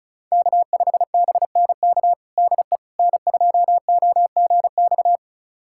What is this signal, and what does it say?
Morse code, heard as a keyed tone, about 36 wpm, 700 Hz. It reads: K5BNK DE N2OGX
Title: USA Callsign DE Callsign - 5 Characters - 35wpm